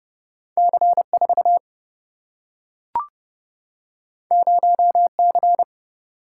Morse code, heard as a keyed tone, about 30 wpm, 700 Hz. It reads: C4 0C
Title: Single-Letter Single-Number 30wpm